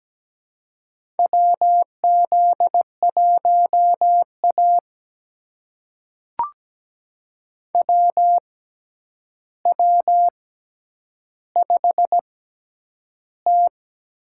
Morse code, heard as a keyed tone, about 17 wpm, spaced as for 9 wpm, 700 Hz. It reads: WZ1A WW5T
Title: USA Call Signs - 6x Character Spacing - 17wpm